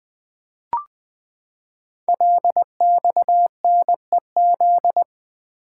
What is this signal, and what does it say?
Morse code, heard as a keyed tone, about 20 wpm, 700 Hz. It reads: LXNEZ
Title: Code Groups - 5 Letters - 20wpm